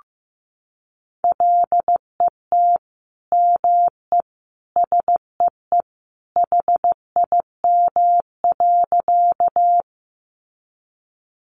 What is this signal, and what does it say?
Morse code, heard as a keyed tone, about 15 wpm, 700 Hz. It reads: LET ME SEE HIM.
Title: Sentences from Top 300 Words 15wpm